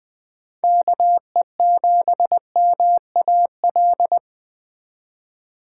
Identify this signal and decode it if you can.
Morse code, heard as a keyed tone, about 20 wpm, 700 Hz. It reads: KE7MAL